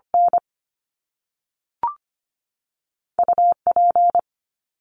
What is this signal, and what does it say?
Morse code, heard as a keyed tone, about 25 wpm, 700 Hz. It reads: N U P
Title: Top Two Letter Combinations in Words 25wpm